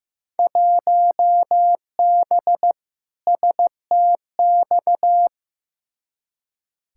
Morse code, heard as a keyed tone, about 15 wpm, 700 Hz. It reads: 1B STX
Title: Field Day Exchanges Encore - 15wpm